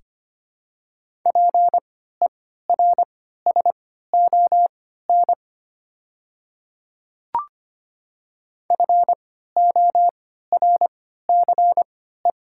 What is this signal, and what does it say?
Morse code, heard as a keyed tone, about 25 wpm, spaced as for 14 wpm, 700 Hz. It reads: PERSON FORCE
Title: 3x Character Spacing - Top 500 Words - 25wpm